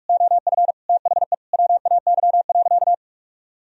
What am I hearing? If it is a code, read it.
Morse code, heard as a keyed tone, about 41 wpm, 700 Hz. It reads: OF THE WAY.
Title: Sentences from Top 1000 Words - Review Part 4 - 45wpm